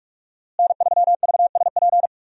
Morse code, heard as a keyed tone, about 42 wpm, 700 Hz. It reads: N3VSP
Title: USA Call Signs - 5 Characters - Repeated 2x - 45wpm